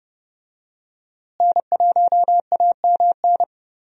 Morse code, heard as a keyed tone, about 30 wpm, 700 Hz. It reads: N1AMN
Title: USA Call Signs - 5 Characters - Repeated 3x - 30wpm